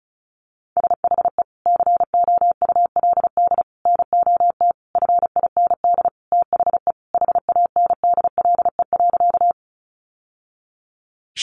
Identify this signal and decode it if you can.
Morse code, heard as a keyed tone, about 34 wpm, 700 Hz. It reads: SHE COULD NOT FIND THE HANDLE.
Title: Sentences from Top 1100 Words 35wpm